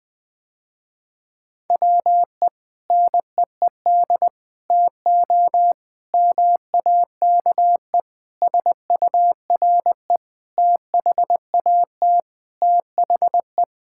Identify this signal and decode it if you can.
Morse code, heard as a keyed tone, about 19 wpm, 700 Hz. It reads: WE NEED TO MAKE SURE THAT THE
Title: Sentences with 5-Word Phrases - Spoken Beginning - 20wpm